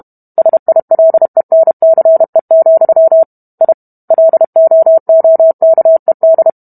Morse code, heard as a keyed tone, about 32 wpm, 650 Hz. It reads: SILENCE, I LOOKED